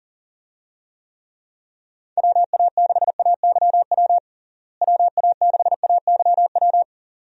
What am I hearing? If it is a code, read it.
Morse code, heard as a keyed tone, about 40 wpm, 700 Hz. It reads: WA6AYW WA6AYW
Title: USA Call Signs - 6 Characters - Repeated 2x - 40wpm